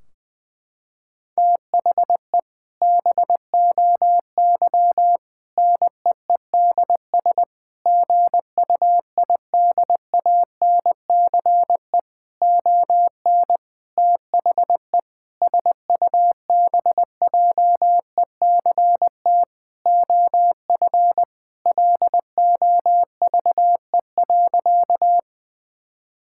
Morse code, heard as a keyed tone, about 20 wpm, 700 Hz. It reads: THE BOY NEEDS GUIDANCE ON THE SUBJECT OF LOVE.
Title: Sentences from Top 1600 Words 20wpm